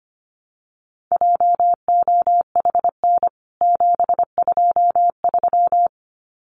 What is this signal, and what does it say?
Morse code, heard as a keyed tone, about 25 wpm, 700 Hz. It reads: JOHN 723
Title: CWT Exchange 25wpm